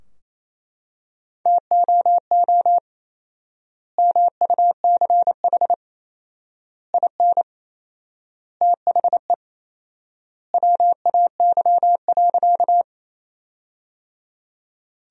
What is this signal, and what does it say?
Morse code, heard as a keyed tone, about 28 wpm, 700 Hz. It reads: TOO MUCH IN THE WAY.
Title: 4x Word-Spacing - Sentences from Top 200 Words - 28wpm